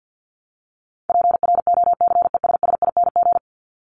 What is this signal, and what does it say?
Morse code, heard as a keyed tone, about 50 wpm, 700 Hz. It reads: PROCESSING
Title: ING Suffix - Top 100 Words with Base Words 50wpm